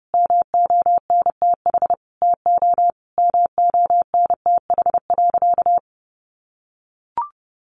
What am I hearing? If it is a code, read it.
Morse code, heard as a keyed tone, about 28 wpm, 700 Hz. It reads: MONTH TO MONTH.
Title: Sentences from Top 1200 Words 30wpm